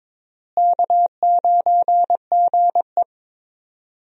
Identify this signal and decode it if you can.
Morse code, heard as a keyed tone, about 22 wpm, 700 Hz. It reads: K9GE
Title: USA Call Signs - 4 Characters - Repeated 3x - 22wpm